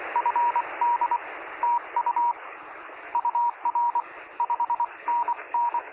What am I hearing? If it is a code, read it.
Morse code, heard as a keyed tone, about 23 wpm, 950 Hz, with radio noise. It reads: FD TU UR 5NN
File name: sm7lcb / 0423z yl2aj 23cm